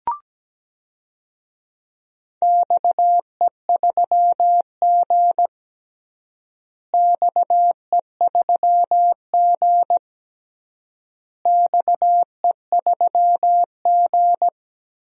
Morse code, heard as a keyed tone, about 17 wpm, 700 Hz. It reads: XE3G XE3G XE3G
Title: DXCC-100 Call Signs - 4 Characters - Repeated 3x - 17wpm